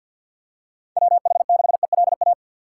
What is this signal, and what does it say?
Morse code, heard as a keyed tone, about 51 wpm, 700 Hz. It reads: WH6ELA
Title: USA Call Signs - 6 Characters - 3x Character Spacing - 50wpm